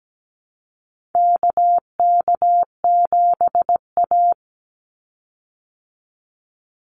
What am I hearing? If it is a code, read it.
Morse code, heard as a keyed tone, about 17 wpm, 700 Hz. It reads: KK7A
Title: Call Signs Repeated 2x - 17wpm